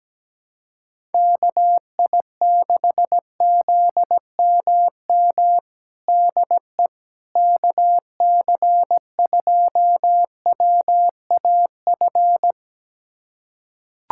Morse code, heard as a keyed tone, about 17 wpm, 700 Hz. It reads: KI6ZMM DE KC2WAF
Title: USA Callsign DE Callsign 17wpm